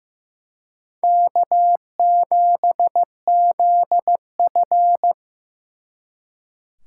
Morse code, heard as a keyed tone, about 18 wpm, 700 Hz. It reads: K7ZF
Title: Speed Racing - USA 1x2 Call Signs 15wpm